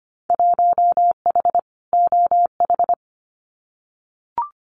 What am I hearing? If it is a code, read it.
Morse code, heard as a keyed tone, about 25 wpm, 700 Hz. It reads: 1H OH
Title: Winter Field Day Exchanges 25wpm